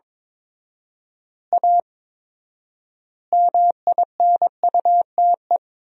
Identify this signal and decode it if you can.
Morse code, heard as a keyed tone, about 22 wpm, 700 Hz. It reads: A MINUTE
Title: 4x Word-Spacing - Sentences from Top 500 Words - 22wpm